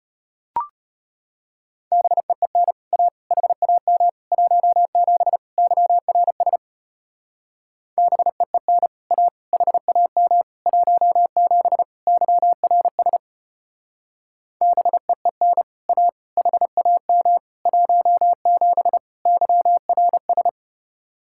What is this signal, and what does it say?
Morse code, heard as a keyed tone, about 29 wpm, 700 Hz. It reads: BEEN A HAM 17 YRS BEEN A HAM 17 YRS BEEN A HAM 17 YRS
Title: Speed-Racing Common QSO Phrases Encore - 25wpm